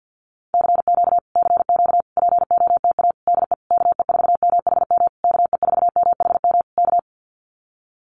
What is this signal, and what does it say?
Morse code, heard as a keyed tone, about 50 wpm, 700 Hz. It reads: CQ CQ POTA DE KE4MHM KE4MHM K
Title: Common QSO Phrases Encore 50wpm